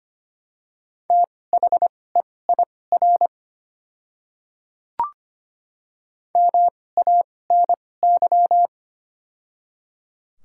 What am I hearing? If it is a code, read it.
Morse code, heard as a keyed tone, about 25 wpm, spaced as for 18 wpm, 700 Hz. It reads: THEIR MANY
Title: 2x Character Spacing - Top 100 Words - 25wpm